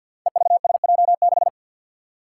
Morse code, heard as a keyed tone, about 53 wpm, 700 Hz. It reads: EVSJ6
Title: Five Challenging Letter-Number Combinations 50wpm